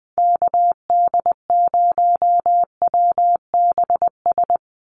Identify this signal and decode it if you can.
Morse code, heard as a keyed tone, about 20 wpm, 700 Hz. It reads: KD0WBS
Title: USA Call Signs 20wpm